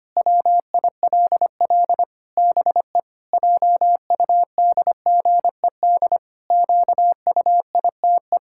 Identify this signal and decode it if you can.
Morse code, heard as a keyed tone, about 25 wpm, 700 Hz. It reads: WILL BE JUDGED QUITE
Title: Pangram Challenge v2 - 25wpm